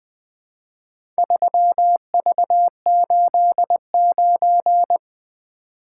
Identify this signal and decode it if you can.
Morse code, heard as a keyed tone, about 20 wpm, 700 Hz. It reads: 3V89
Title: Four Letter-Number Combinations - 20wpm